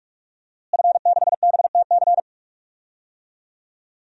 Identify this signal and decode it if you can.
Morse code, heard as a keyed tone, about 46 wpm, 700 Hz. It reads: F6BTC
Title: DXCC-100 Call Signs - 5 Characters - Repeated 2x - 45wpm